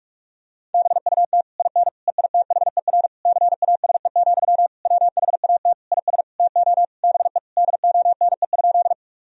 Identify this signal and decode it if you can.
Morse code, heard as a keyed tone, about 46 wpm, 700 Hz. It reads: BUT IN EITHER CASE, WHAT IS TO BE DONE?
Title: Sentences from Top 1000 Words - Review Part 4 - 45wpm